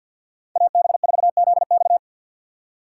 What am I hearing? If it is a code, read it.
Morse code, heard as a keyed tone, about 47 wpm, 700 Hz. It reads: AB4ZX
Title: USA Call Signs - 5 Characters - 6x Character Spacing - 50wpm